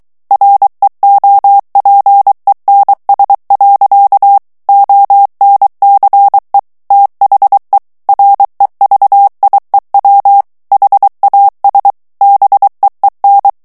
Morse code, heard as a keyed tone, about 23 wpm, 800 Hz. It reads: REOPENS. ONCE THE REVIEW HAS BEEN
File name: ANGLAIS texte clair 3 vitesse 22 mots minute